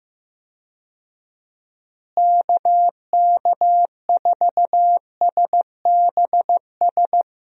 Morse code, heard as a keyed tone, about 15 wpm, 700 Hz. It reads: KK4SBS
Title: USA Call Signs - 6 Characters - Repeated 3x - 15wpm